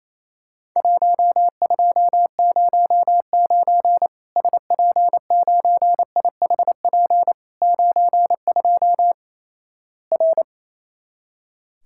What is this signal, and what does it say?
Morse code, heard as a keyed tone, about 28 wpm, 700 Hz. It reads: 1209 SP9IHP 92 R
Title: Worked All Europe DX Contest - QTC Exchange - 28wpm